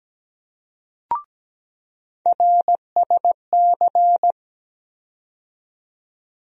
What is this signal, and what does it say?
Morse code, heard as a keyed tone, about 17 wpm, 700 Hz. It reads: RSC
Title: USA Call Sign Postfixes - 3 Letters 17wpm